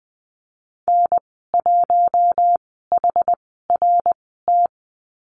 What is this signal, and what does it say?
Morse code, heard as a keyed tone, about 20 wpm, spaced as for 15 wpm, 700 Hz. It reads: N1HRT
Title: 062 Lesson 11g - Letter h - Call Signs 20wpm